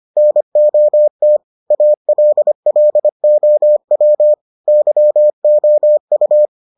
Morse code, heard as a keyed tone, about 25 wpm, 600 Hz. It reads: NOT ALLOW YOU